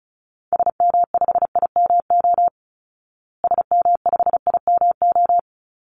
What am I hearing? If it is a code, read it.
Morse code, heard as a keyed tone, about 34 wpm, 700 Hz. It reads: SM5IMO SM5IMO
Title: Call Signs Repeated 3x - 35wpm